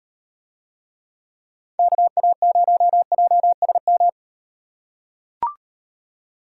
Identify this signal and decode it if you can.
Morse code, heard as a keyed tone, about 37 wpm, 700 Hz. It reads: KA0JSM